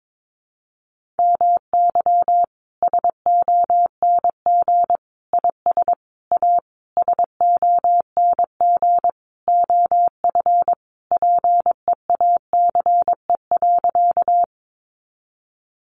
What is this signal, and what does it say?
Morse code, heard as a keyed tone, about 22 wpm, 700 Hz. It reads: MY SONG IS A SONG OF PEACE.
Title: Sentences from Top 1500 Words 22wpm